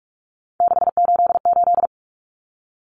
Morse code, heard as a keyed tone, about 44 wpm, 700 Hz. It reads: 688
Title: Numbers - 3-Digit 45wpm